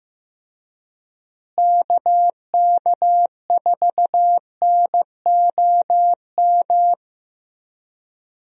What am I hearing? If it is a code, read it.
Morse code, heard as a keyed tone, about 15 wpm, 700 Hz. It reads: KK4NOM
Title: USA Call Signs - 6 Characters - Repeated 3x - 15wpm